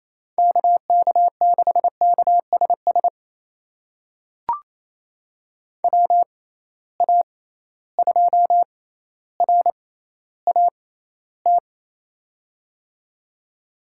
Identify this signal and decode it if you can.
Morse code, heard as a keyed tone, about 28 wpm, spaced as for 14 wpm, 700 Hz. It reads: KK6KSS WA2RAT
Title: USA Call Signs - 6 Characters - 6x Character Spacing - 28wpm